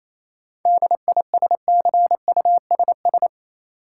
Morse code, heard as a keyed tone, about 28 wpm, 700 Hz. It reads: DISCUSS
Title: Sentences from Top 1000 Words - New Word Drill - 28wpm